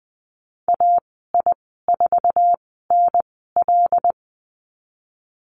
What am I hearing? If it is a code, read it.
Morse code, heard as a keyed tone, about 20 wpm, spaced as for 15 wpm, 700 Hz. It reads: AI4NL
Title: 088 Lesson 15g - Number 5 - Call Signs 20wpm